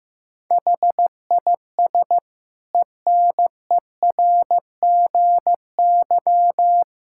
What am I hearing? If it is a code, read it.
Morse code, heard as a keyed tone, about 15 wpm, 700 Hz. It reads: HIS ENERGY
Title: Sentences from Top 1000 Words - Review Part 6 - 15wpm